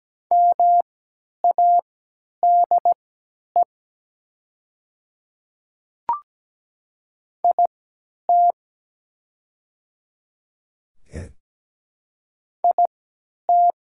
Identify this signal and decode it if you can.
Morse code, heard as a keyed tone, about 17 wpm, spaced as for 10 wpm, 700 Hz. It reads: MADE IT IT
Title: 3x Character Spacing - Top 200 Words - 17wpm